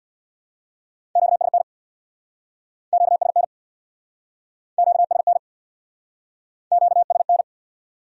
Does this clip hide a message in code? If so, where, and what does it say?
Yes: Morse code from 1.2 to 7.4 s, heard as a keyed tone, about 64 wpm, 700 Hz. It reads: QSN QSN QSN QSN